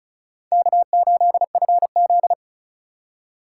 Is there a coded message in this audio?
Yes: Morse code from 0.5 to 2.3 s, heard as a keyed tone, about 45 wpm, 700 Hz. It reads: K8FZ